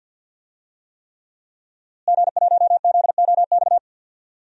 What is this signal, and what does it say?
Morse code, heard as a keyed tone, about 52 wpm, 700 Hz. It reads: G1ZOX
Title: DXCC-100 Call Signs - 5 Characters - Repeated 3x - 50wpm